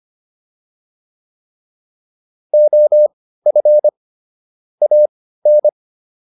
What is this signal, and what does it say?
Morse code, heard as a keyed tone, about 25 wpm, spaced as for 15 wpm, 600 Hz. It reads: OF AN